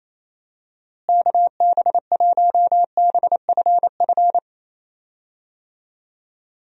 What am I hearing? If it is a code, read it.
Morse code, heard as a keyed tone, about 27 wpm, 700 Hz. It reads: KB1BFF
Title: USA Call Signs - 6 Characters - Repeated 3x - 28wpm